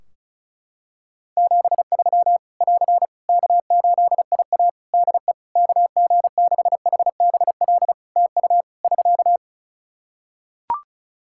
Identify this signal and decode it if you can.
Morse code, heard as a keyed tone, about 34 wpm, 700 Hz. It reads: 73 <AR> K8IA DE KG6HBL TU <SK>
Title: Practice QSO - Exchange 4 - Ending - 35wpm